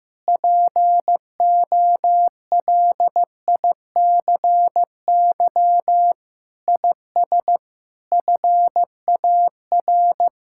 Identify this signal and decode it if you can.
Morse code, heard as a keyed tone, about 16 wpm, 700 Hz. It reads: POLICY IS FAR
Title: Sentences from Top 1000 Words - Review Part 5 - 15wpm